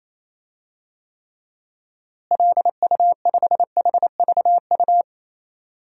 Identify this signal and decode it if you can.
Morse code, heard as a keyed tone, about 28 wpm, 700 Hz. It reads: LU5HVU
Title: DXCC-100 Call Signs - 6 Characters - Repeated 2x - 28wpm